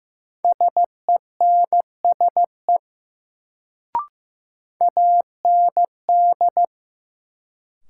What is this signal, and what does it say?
Morse code, heard as a keyed tone, about 15 wpm, 700 Hz. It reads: SENSE AND
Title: Sets of 2 Words - New Word Drill - 15wpm